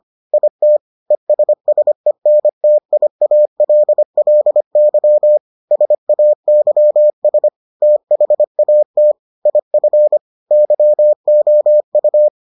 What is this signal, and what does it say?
Morse code, heard as a keyed tone, about 25 wpm, 600 Hz. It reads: IT ESSENTIALLY SAYS THAT IF YOU